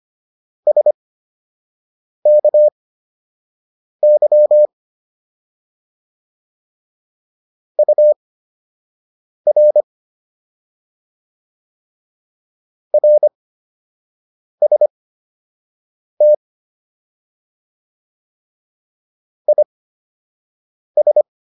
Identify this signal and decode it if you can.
Morse code, heard as a keyed tone, about 25 wpm, spaced as for 6 wpm, 600 Hz. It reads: SKY UR RST IS